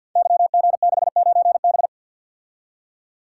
Morse code, heard as a keyed tone, about 51 wpm, 700 Hz. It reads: YG69B
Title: Code Groups - 5 Letter:Numbers - 50wpm